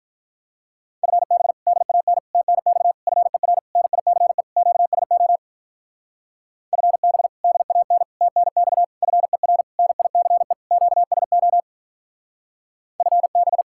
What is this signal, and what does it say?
Morse code, heard as a keyed tone, about 44 wpm, 700 Hz. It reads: FB DAN TNX FER NICE QSO FB DAN TNX FER NICE QSO FB
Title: Speed-Racing Common QSO Phrases - 35wpm